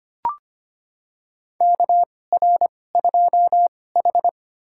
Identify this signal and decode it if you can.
Morse code, heard as a keyed tone, about 25 wpm, spaced as for 18 wpm, 700 Hz. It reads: KR2H
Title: USA Call Signs - 4 Characters - 2x Character Spacing - 25wpm